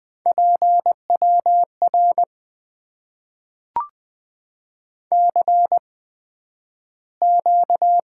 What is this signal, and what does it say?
Morse code, heard as a keyed tone, about 20 wpm, spaced as for 11 wpm, 700 Hz. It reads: PWR CQ
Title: Top 90 QSO Elements - 8x Character Spacing - 20wpm